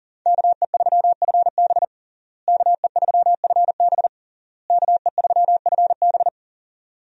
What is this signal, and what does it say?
Morse code, heard as a keyed tone, about 40 wpm, 690 Hz. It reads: KE3FB KE3FB KE3FB
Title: USA Call Signs - 5 Characters - Repeated 3x - 40wpm